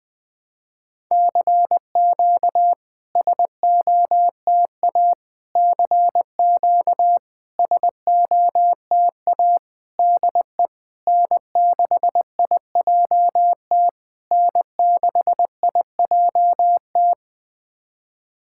Morse code, heard as a keyed tone, about 20 wpm, 690 Hz. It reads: CQ SOTA CQ SOTA DE N6IJT N6IJT
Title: SOTA QSO Practice 20wpm